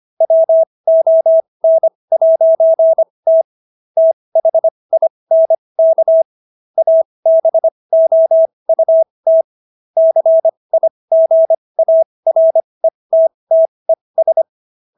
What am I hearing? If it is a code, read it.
Morse code, heard as a keyed tone, about 25 wpm, spaced as for 20 wpm, 650 Hz. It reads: WON'T THINK ABOUT CIGARETTES